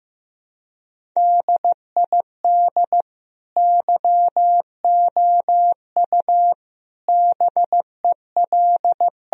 Morse code, heard as a keyed tone, about 16 wpm, 700 Hz. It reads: DID YOU BEL
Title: Sentences from Top 1000 Words - Review Part 1 - 15wpm